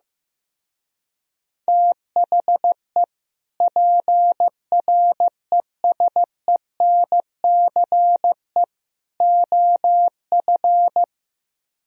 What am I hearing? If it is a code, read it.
Morse code, heard as a keyed tone, about 15 wpm, 700 Hz. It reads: THE PRESENCE OF
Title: Speed-Racing - Sets of 3 Words 15wpm